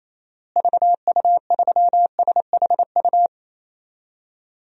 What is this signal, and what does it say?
Morse code, heard as a keyed tone, about 28 wpm, 700 Hz. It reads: VU3SHU
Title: DXCC-100 Call Signs - 6 Characters - Repeated 2x - 28wpm